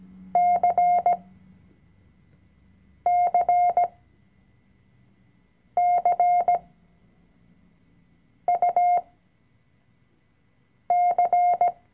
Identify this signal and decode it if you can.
Morse code, heard as a keyed tone, about 17 wpm, spaced as for 4 wpm, 710 Hz. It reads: CCCUC
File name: CW-Online-Kurs / Anfänger / Lektion 004